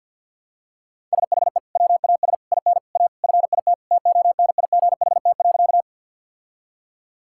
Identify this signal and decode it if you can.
Morse code, heard as a keyed tone, about 52 wpm, 690 Hz. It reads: SHE WAS IN A FIT TONIGHT.
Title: Sentences from Top 1000 Words - Review Part 5 - 50wpm